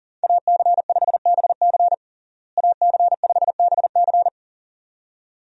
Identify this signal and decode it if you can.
Morse code, heard as a keyed tone, about 40 wpm, 700 Hz. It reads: AC5BC AC5BC
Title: USA Call Signs - 5 Characters - Repeated 2x - 40wpm